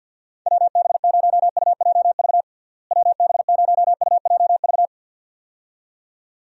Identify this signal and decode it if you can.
Morse code, heard as a keyed tone, about 49 wpm, 700 Hz. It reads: WB0UJV WB0UJV
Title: USA Call Signs - 6 Characters - Repeated 2x - 50wpm